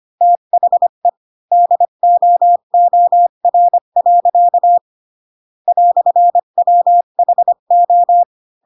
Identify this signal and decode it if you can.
Morse code, heard as a keyed tone, about 25 wpm, 700 Hz. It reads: THE DOOR. "WHO